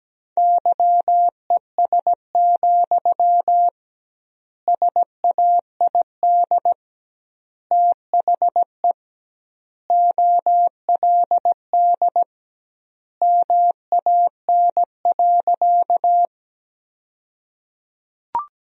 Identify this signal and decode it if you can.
Morse code, heard as a keyed tone, about 17 wpm, 700 Hz. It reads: YES, SAID THE OLD MAN.
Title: 2x Word-Spacing - Sentences from Top 500 Words - 17wpm